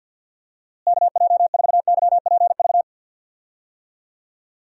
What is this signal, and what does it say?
Morse code, heard as a keyed tone, about 49 wpm, 700 Hz. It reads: KJ4YPV
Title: USA Call Signs - 6 Characters - Repeated 3x - 50wpm